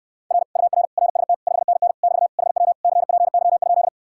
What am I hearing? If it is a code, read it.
Morse code, heard as a keyed tone, about 69 wpm, 700 Hz. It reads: R UR RST 5NN <BT> HW CPY?